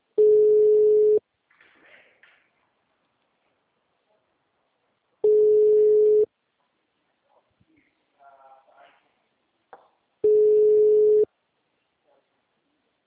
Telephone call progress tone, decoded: ringback tone

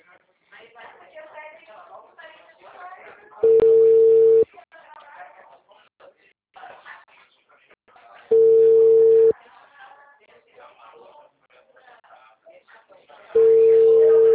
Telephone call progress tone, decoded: ringback tone

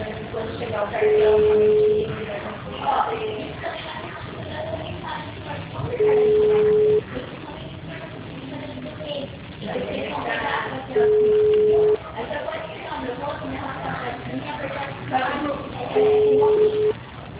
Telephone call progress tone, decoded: ringback tone

